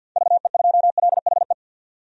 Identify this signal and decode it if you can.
Morse code, heard as a keyed tone, about 53 wpm, 700 Hz. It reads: VE2LHE
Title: DXCC-100 Call Signs - 6 Characters - 50wpm